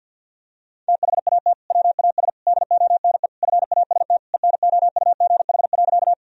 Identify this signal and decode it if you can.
Morse code, heard as a keyed tone, about 50 wpm, 700 Hz. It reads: THAT WAS DONE FAST ENOUGH.
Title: Sentences from Top 1000 Words - Review Part 2 - 50wpm